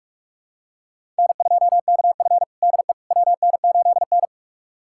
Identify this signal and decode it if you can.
Morse code, heard as a keyed tone, about 46 wpm, 700 Hz. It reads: N2KF DE WN8N
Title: USA Callsign DE Callsign - 4 Characters - 45wpm